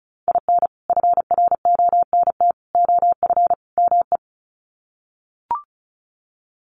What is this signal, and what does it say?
Morse code, heard as a keyed tone, about 34 wpm, 700 Hz. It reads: IN FRONT OF ME
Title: Sets of 4 Words 35wpm